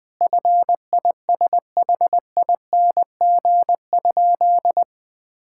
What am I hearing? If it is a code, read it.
Morse code, heard as a keyed tone, about 20 wpm, 700 Hz. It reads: FISHING?
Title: Sentences from Top 1000 Words - Review Part 5 - 20wpm